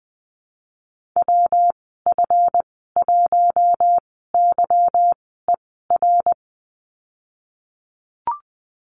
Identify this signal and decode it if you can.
Morse code, heard as a keyed tone, about 20 wpm, spaced as for 15 wpm, 700 Hz. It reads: WF1YER
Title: 161 Lesson 26g - Letter g - Call Signs 20wpm